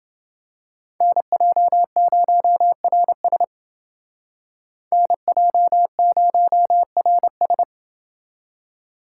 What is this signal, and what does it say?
Morse code, heard as a keyed tone, about 28 wpm, 700 Hz. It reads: NJ0RS NJ0RS